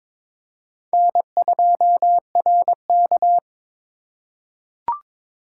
Morse code, heard as a keyed tone, about 22 wpm, 700 Hz. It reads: N2RK
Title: USA Call Signs - 4 Characters - 8x Character Spacing - 22wpm